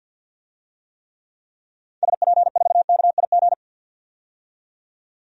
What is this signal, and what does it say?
Morse code, heard as a keyed tone, about 48 wpm, 700 Hz. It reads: SP4KIG
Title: DXCC-100 Call Signs - 6 Characters - 50wpm